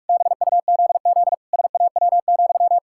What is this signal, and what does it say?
Morse code, heard as a keyed tone, about 46 wpm, 700 Hz. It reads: BUZZ SAW,
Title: Pangram Challenge v2 - 45wpm